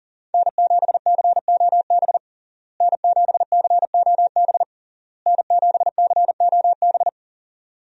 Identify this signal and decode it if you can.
Morse code, heard as a keyed tone, about 40 wpm, 700 Hz. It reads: N7COB N7COB N7COB
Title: USA Call Signs - 5 Characters - Repeated 3x - 40wpm